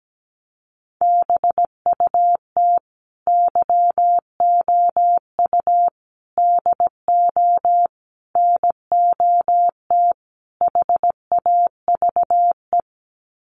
Morse code, heard as a keyed tone, about 17 wpm, 690 Hz. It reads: BUT YOU DO NOT HAVE
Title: Sets of 5 Words Encore 17wpm